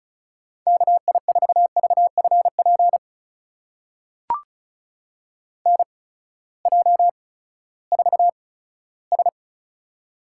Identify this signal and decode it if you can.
Morse code, heard as a keyed tone, about 34 wpm, spaced as for 17 wpm, 700 Hz. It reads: KI4VFP NJ4S
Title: USA Call Signs - 8x Character Spacing - 35wpm